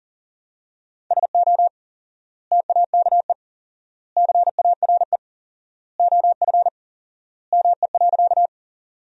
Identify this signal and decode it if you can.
Morse code, heard as a keyed tone, about 39 wpm, 700 Hz. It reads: SO TAKE CARE OF ME.
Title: 4x Word-Spacing - Sentences from Top 300 Words - 40wpm